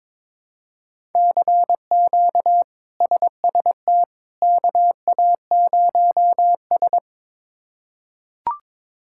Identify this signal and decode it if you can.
Morse code, heard as a keyed tone, about 22 wpm, 700 Hz. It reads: CQ SST KA0S
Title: K1USN SST QSO Practice - 22wpm